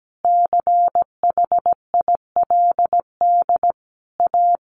Morse code, heard as a keyed tone, about 17 wpm, 700 Hz. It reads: CHILD A
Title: Sentences from Top 800 Words 17wpm